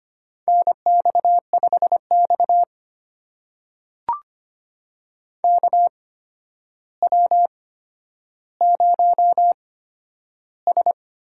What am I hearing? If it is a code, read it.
Morse code, heard as a keyed tone, about 25 wpm, spaced as for 11 wpm, 700 Hz. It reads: NX5X KW0S
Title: USA Call Signs - 4 Characters - 8x Character Spacing - 25wpm